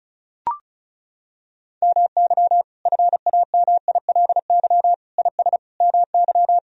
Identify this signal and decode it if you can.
Morse code, heard as a keyed tone, about 36 wpm, 700 Hz. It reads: MY FAMILY IS MY
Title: Sentences from Top 1000 Words - Review Part 4 - 35wpm